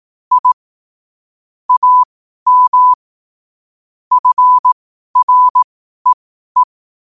Morse code, heard as a keyed tone, about 18 wpm, spaced as for 12 wpm, 1000 Hz. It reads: I AM FREE
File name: qotdfiles / 12wpm